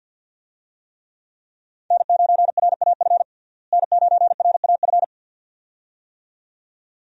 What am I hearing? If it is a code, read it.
Morse code, heard as a keyed tone, about 49 wpm, 700 Hz. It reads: N9RAF N9RAF
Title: USA Call Signs - 5 Characters - Repeated 2x - 50wpm